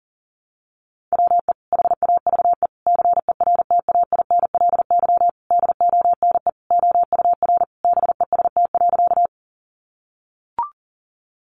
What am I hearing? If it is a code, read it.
Morse code, heard as a keyed tone, about 40 wpm, 700 Hz. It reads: WE HAVE CERTAINLY DONE OUR BEST.
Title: Sentences from Top 900 Words 40wpm